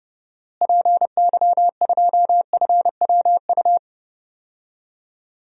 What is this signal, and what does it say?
Morse code, heard as a keyed tone, about 30 wpm, 700 Hz. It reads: PY2FWU
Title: DXCC-100 Call Signs - 6 Characters - Repeated 2x - 30wpm